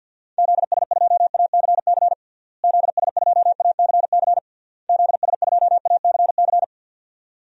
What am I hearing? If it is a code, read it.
Morse code, heard as a keyed tone, about 50 wpm, 700 Hz. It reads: ZS2ACC ZS2ACC ZS2ACC
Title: DXCC-100 Call Signs - 6 Characters - Repeated 3x - 50wpm